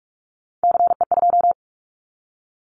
Morse code, heard as a keyed tone, about 44 wpm, 700 Hz. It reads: CE2
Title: Callsign Prefixes for Top 100 DXCC - 45wpm